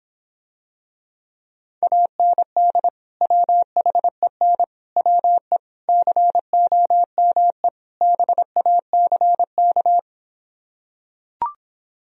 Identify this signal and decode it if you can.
Morse code, heard as a keyed tone, about 26 wpm, 700 Hz. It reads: AND WHEN WE COME BACK